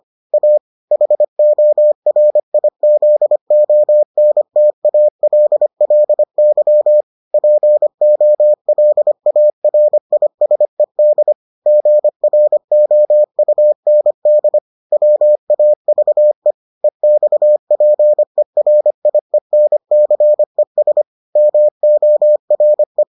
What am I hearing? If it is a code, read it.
Morse code, heard as a keyed tone, about 25 wpm, 600 Hz. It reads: A HORIZONTALLY POLARISED GROUND WAVE EXPERIENCES MORE